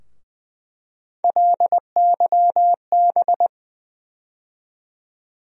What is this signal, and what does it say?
Morse code, heard as a keyed tone, about 20 wpm, 700 Hz. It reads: LYB
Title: USA Call Sign Postfixes - 3 Letters 20wpm